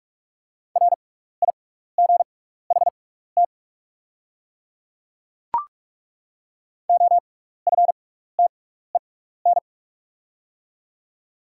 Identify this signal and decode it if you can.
Morse code, heard as a keyed tone, about 42 wpm, spaced as for 15 wpm, 700 Hz. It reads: RIGHT OFTEN
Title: 6x Character Spacing - Top 500 Words - 45wpm